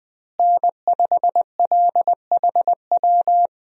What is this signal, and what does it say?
Morse code, heard as a keyed tone, about 20 wpm, 700 Hz. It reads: N5LHW
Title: USA Call Signs - 5 Characters - Repeated 3x - 20wpm